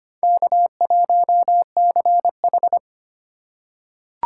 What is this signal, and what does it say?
Morse code, heard as a keyed tone, about 25 wpm, 700 Hz. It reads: K1CH
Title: USA Call Signs - 4x Character Spacing - 25wpm